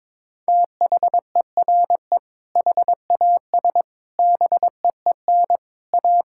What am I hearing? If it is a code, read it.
Morse code, heard as a keyed tone, about 22 wpm, 700 Hz. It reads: THERE HAS BEEN A
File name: practice / 22wpm / Sets of 8 Words - 22wpm